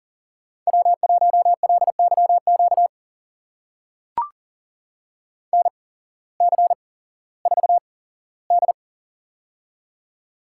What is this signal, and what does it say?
Morse code, heard as a keyed tone, about 40 wpm, spaced as for 19 wpm, 700 Hz. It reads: W1LYQ NC4D
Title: USA Call Signs - 8x Character Spacing - 40wpm